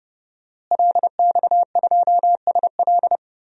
Morse code, heard as a keyed tone, about 29 wpm, 700 Hz. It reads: LX2SL
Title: DXCC-100 Call Signs - 5 Characters - Repeated 3x - 30wpm